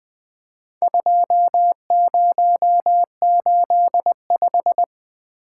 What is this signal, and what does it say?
Morse code, heard as a keyed tone, about 20 wpm, 700 Hz. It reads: 2085
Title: Four Letter-Number Combinations - 20wpm